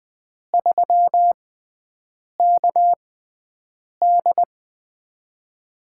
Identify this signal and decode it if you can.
Morse code, heard as a keyed tone, about 20 wpm, spaced as for 7 wpm, 700 Hz. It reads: 3KD
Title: USA Call Signs - 4 Characters - 6x Character Spacing - 20wpm